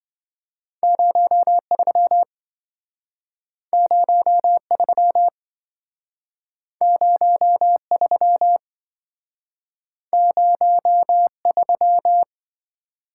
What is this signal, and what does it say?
Morse code, heard as a keyed tone, about 28 wpm, 700 Hz. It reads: 03 03 03 03